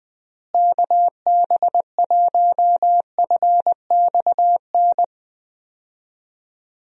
Morse code, heard as a keyed tone, about 20 wpm, 700 Hz. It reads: KB1FXN
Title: USA Call Signs - 6 Characters - Repeated 3x - 20wpm